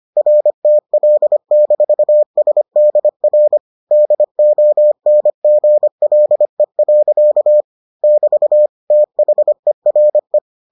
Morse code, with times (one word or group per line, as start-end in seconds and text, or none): 0.2-3.6 RTL-SDR
3.9-7.6 DONGLE.
8.0-8.7 <BT>
8.9-10.4 THERE